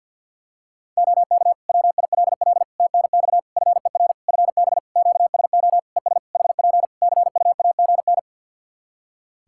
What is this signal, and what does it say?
Morse code, heard as a keyed tone, about 51 wpm, 700 Hz. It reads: OK WILL TNX FER FB QSO ES HP CUAGN
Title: Practice QSO - Exchange 4 - Ending - 50wpm